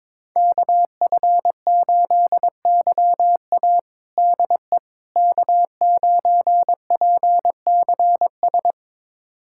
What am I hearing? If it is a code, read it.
Morse code, heard as a keyed tone, about 22 wpm, 700 Hz. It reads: KF8YA DE K9PCS
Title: USA Callsign DE Callsign - 5 Characters - 22wpm